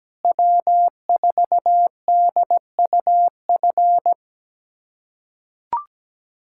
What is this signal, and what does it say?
Morse code, heard as a keyed tone, about 17 wpm, 700 Hz. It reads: W4DUF
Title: USA Callsign DE Callsign - 5 Characters - 17wpm